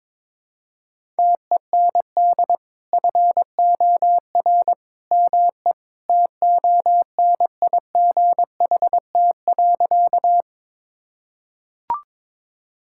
Morse code, heard as a keyed tone, about 22 wpm, 700 Hz. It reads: TEND FOR ME TONIGHT.
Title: Sentences from Top 1000 Words - Review Part 5 - 22wpm